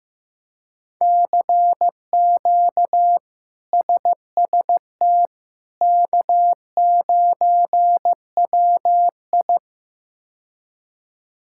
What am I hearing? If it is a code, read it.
Morse code, heard as a keyed tone, about 15 wpm, 700 Hz. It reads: CQ SST K9WI
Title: K1USN SST QSO Practice - 15wpm